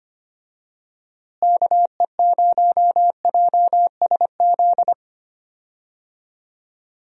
Morse code, heard as a keyed tone, about 25 wpm, 700 Hz. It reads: KE0JSZ
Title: USA Call Signs - 6 Characters - Repeated 2x - 25wpm